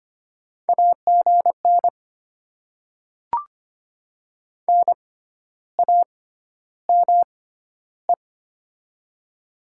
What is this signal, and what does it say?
Morse code, heard as a keyed tone, about 25 wpm, spaced as for 12 wpm, 700 Hz. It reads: AGN NAME
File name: Top 20 QSO Elements - 6x Character Spacing - 25wpm